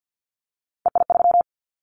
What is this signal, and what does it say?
Morse code, heard as a keyed tone, about 50 wpm, 700 Hz. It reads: EI3
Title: Callsign Prefixes for Top 50 DXCC - 50wpm